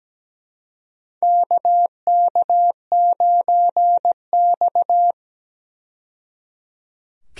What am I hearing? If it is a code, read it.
Morse code, heard as a keyed tone, about 17 wpm, 700 Hz. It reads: KK9X
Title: USA Call Signs - 4 Characters - 17wpm